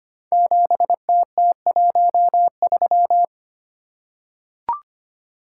Morse code, heard as a keyed tone, about 25 wpm, 700 Hz. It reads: 7TT13
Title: Code Groups - 5 Letter:Numbers - 25wpm